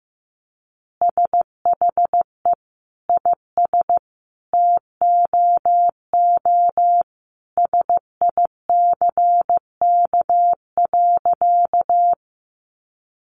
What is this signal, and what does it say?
Morse code, heard as a keyed tone, about 15 wpm, 700 Hz. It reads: SHE IS TOO SICK.
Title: Easy Phrases 15wpm